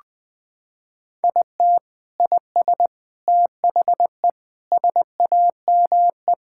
Morse code, heard as a keyed tone, about 20 wpm, 700 Hz. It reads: IT IS THE SAME
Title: Sentences with 6-Word Phrases - 20wpm